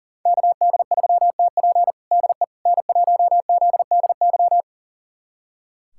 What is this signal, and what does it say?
Morse code, heard as a keyed tone, about 40 wpm, 700 Hz. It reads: KD3TP DE N1ZDY
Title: USA Callsign DE Callsign - 5 Characters - 40wpm